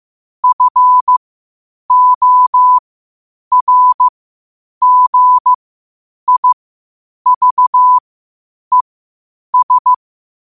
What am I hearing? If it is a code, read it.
Morse code, heard as a keyed tone, about 15 wpm, spaced as for 8 wpm, 1000 Hz. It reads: FORGIVES
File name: qotdfiles / 8wpm